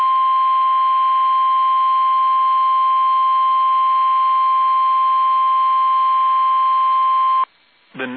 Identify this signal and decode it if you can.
Emergency Alert System (SAME) tones (the 1050 Hz attention tone), with speech. It is heard from 0.6 to 7.2 s.
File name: eas / KUSO / KUSO